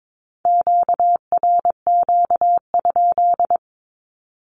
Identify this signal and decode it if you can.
Morse code, heard as a keyed tone, about 22 wpm, 700 Hz. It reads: QRQ?
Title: Q-Codes with Spoken Meaning - 22wpm